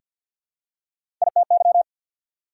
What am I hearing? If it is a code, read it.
Morse code, heard as a keyed tone, about 49 wpm, 690 Hz. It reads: ITY
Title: Prefixes and Suffixes 50wpm